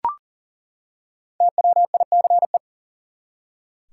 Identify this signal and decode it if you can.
Morse code, heard as a keyed tone, about 40 wpm, 700 Hz. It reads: TWICE
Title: Five Letter Words - 40wpm